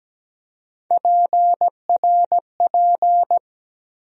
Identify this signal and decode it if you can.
Morse code, heard as a keyed tone, about 17 wpm, 700 Hz. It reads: PRP
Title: USA Call Sign Postfixes - 3 Letters 17wpm